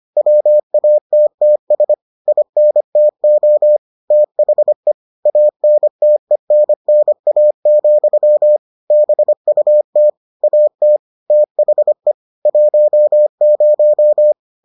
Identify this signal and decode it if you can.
Morse code, heard as a keyed tone, about 25 wpm, 600 Hz. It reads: WATTS INTO THE ANTENNA, BUT AT THE 10